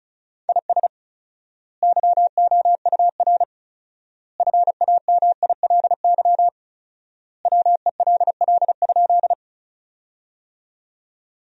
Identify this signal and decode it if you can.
Morse code, heard as a keyed tone, about 36 wpm, 700 Hz. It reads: IS YOUR FAMILY WELL?
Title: 4x Word-Spacing - Sentences from Top 400 Words - 35wpm